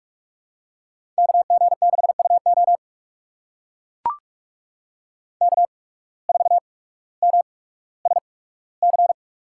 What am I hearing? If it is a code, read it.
Morse code, heard as a keyed tone, about 41 wpm, spaced as for 19 wpm, 700 Hz. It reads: KG6UO K4MSC
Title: USA Call Signs - 5 Characters - 8x Character Spacing - 45wpm